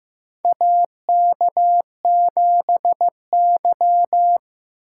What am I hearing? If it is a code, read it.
Morse code, heard as a keyed tone, about 15 wpm, 700 Hz. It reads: AK7Y
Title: USA Call Signs - 4 Characters - 8x Character Spacing - 15wpm